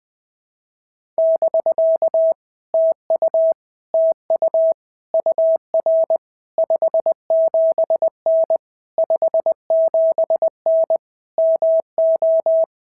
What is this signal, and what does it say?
Morse code, heard as a keyed tone, about 20 wpm, 650 Hz. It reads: <BK> TU TU UR 57N 57N MO
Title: POTA QSO Practice - 20wpm